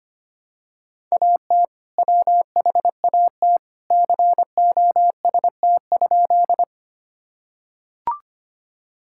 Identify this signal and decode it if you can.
Morse code, heard as a keyed tone, about 25 wpm, 700 Hz. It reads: AT WHAT COST?
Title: Sentences from Top 1000 Words - Review Part 5 - 25wpm